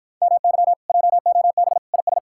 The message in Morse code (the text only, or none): MY JOB IS